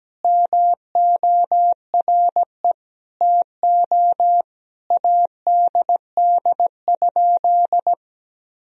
Morse code, heard as a keyed tone, about 17 wpm, 700 Hz. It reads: MORE TO ADD?
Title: Sentences from Top 1000 Words - Review Part 1 - 17wpm